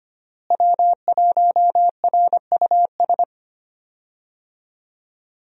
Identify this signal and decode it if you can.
Morse code, heard as a keyed tone, about 25 wpm, 700 Hz. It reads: W1RUS
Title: USA Call Signs - 5 Characters - Repeated 2x - 25wpm